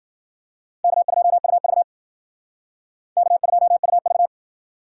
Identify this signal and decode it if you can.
Morse code, heard as a keyed tone, about 53 wpm, 700 Hz. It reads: K2UV K2UV